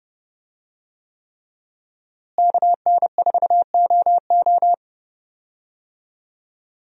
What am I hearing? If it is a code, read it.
Morse code, heard as a keyed tone, about 32 wpm, 700 Hz. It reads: KN4OO
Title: USA Call Signs - 5 Characters - Repeated 3x - 30wpm